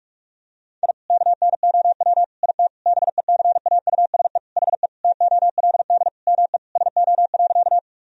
Morse code, heard as a keyed tone, about 46 wpm, 700 Hz. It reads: I KNOW IT BECAUSE HE TOLD ME SO.
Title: Sentences from Top 1000 Words - Review Part 3 - 45wpm